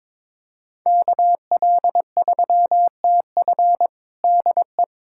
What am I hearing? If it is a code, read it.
Morse code, heard as a keyed tone, about 22 wpm, 700 Hz. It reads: KL3TF DE
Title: USA Callsign DE Callsign - 5 Characters - 22wpm